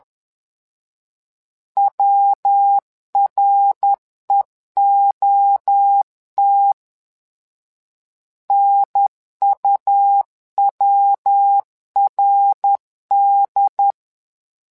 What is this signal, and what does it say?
Morse code, heard as a keyed tone, about 11 wpm, 800 Hz. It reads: WREOT NUWRD
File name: LEZIONI / 3 MWRO / 3 MWRO 10wpm 2148